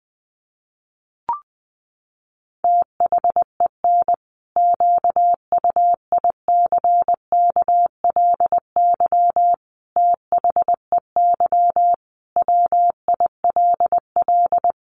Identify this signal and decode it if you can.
Morse code, heard as a keyed tone, about 20 wpm, 700 Hz. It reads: THEN QUICKLY THEY WILL
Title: Sentences from Top 700 Words 20wpm